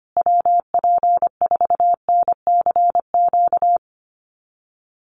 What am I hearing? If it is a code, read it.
Morse code, heard as a keyed tone, about 25 wpm, 700 Hz. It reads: WP4NCQ
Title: USA Call Signs Repeated 2x 25wpm